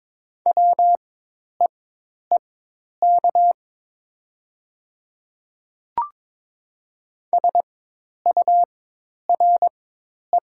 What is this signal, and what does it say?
Morse code, heard as a keyed tone, about 22 wpm, spaced as for 10 wpm, 700 Hz. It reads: WEEK SURE
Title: 4x Character Spacing - Top 500 Words - 22wpm